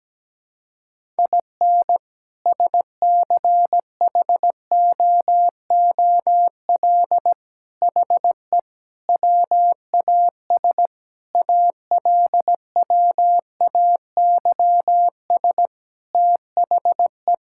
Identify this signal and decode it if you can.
Morse code, heard as a keyed tone, about 17 wpm, 700 Hz. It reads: IN SCHOOL HE WAS ALWAYS THE
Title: Sentences from Top 1000 Words - Review Part 5 - 17wpm